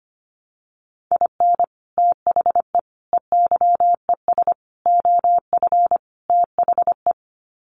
Morse code, heard as a keyed tone, about 25 wpm, 700 Hz. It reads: IN THE EYES OF THE
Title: Sets of 5 Words 25wpm